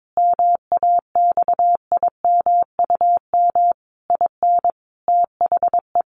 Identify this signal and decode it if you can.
Morse code, heard as a keyed tone, about 22 wpm, 700 Hz. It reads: MAXIMUM IN THE
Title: Sentences from Top 1300 Words 22wpm